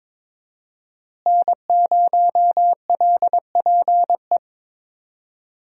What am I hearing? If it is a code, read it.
Morse code, heard as a keyed tone, about 21 wpm, 700 Hz. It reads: N0LPE
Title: USA Call Signs - 5 Characters - Repeated 2x - 22wpm